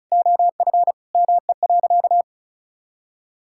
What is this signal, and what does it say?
Morse code, heard as a keyed tone, about 36 wpm, 700 Hz. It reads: OF ME.
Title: Sentences from Top 1000 Words - Review Part 6 - 35wpm